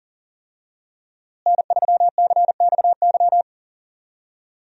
Morse code, heard as a keyed tone, about 40 wpm, 700 Hz. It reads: N3CXY
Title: USA Call Signs - 5 Characters - Repeated 3x - 40wpm